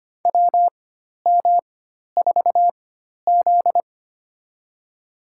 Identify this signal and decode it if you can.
Morse code, heard as a keyed tone, about 25 wpm, spaced as for 12 wpm, 690 Hz. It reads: WM4Z
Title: USA Call Signs - 4 Characters - 4x Character Spacing - 25wpm